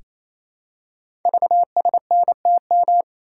Morse code, heard as a keyed tone, about 28 wpm, 700 Hz. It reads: VSNTM
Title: Code Groups - 5 Letter:Numbers - 28wpm